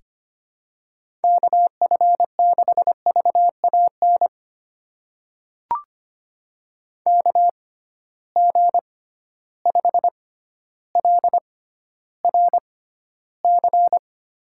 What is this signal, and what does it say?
Morse code, heard as a keyed tone, about 25 wpm, spaced as for 13 wpm, 700 Hz. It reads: KF6VAN KG5LRC
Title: USA Call Signs - 6 Characters - 6x Character Spacing - 25wpm